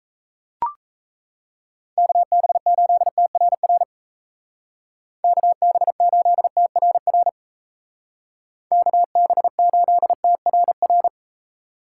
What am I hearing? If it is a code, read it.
Morse code, heard as a keyed tone, about 34 wpm, 700 Hz. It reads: KB8TRR KB8TRR KB8TRR
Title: Speed Racing - USA 2x3 Call Signs 28wpm